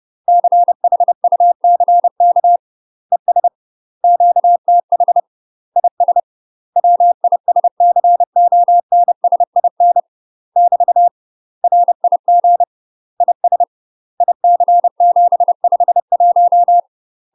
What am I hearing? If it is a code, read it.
Morse code, heard as a keyed tone, about 30 wpm, 700 Hz. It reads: CHUCK ES QTH IS WISCONSIN <BT> RIG IS IC751
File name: UK MASTER QSOs TRAINING / 30WPM / UK MASTER QSOs 0046